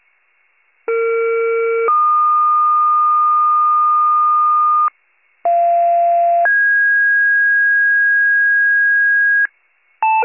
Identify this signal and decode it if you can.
Two-tone page, heard as a one-second tone, then a three-second tone, about 460 then 1220 Hz, about 690 then 1650 Hz, with noise. It is heard from 0.9 to 9.5 s.